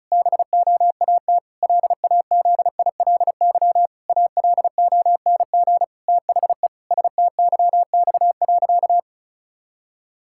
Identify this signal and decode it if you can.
Morse code, heard as a keyed tone, about 36 wpm, 700 Hz. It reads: BOAT LAZILY ALONG THE STYX.
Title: Pangram Challenge v2 - 35wpm